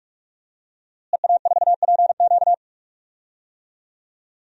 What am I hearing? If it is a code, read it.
Morse code, heard as a keyed tone, about 43 wpm, 700 Hz. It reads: EA4PQ
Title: DXCC-100 Call Signs - 5 Characters - Repeated 2x - 45wpm